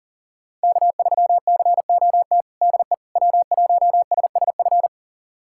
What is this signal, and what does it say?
Morse code, heard as a keyed tone, about 37 wpm, 700 Hz. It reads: K3COT DE W1SSF
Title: USA Callsign DE Callsign 40wpm